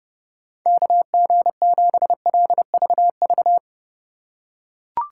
Morse code, heard as a keyed tone, about 27 wpm, 700 Hz. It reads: KG7LVV